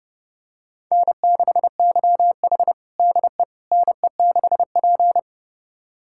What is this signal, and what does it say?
Morse code, heard as a keyed tone, about 30 wpm, 700 Hz. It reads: N6YH DE NE6P
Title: USA Callsign DE Callsign - 4 Characters - 30wpm